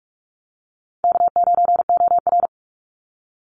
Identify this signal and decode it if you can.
Morse code, heard as a keyed tone, about 46 wpm, 700 Hz. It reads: K9OR
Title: Call Signs - 45wpm